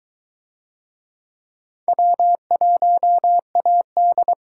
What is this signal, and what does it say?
Morse code, heard as a keyed tone, about 22 wpm, 700 Hz. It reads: W1AD